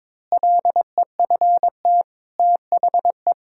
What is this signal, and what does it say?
Morse code, heard as a keyed tone, about 22 wpm, 700 Hz. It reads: LEFT THE
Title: Pangram Challenge v2 - 22wpm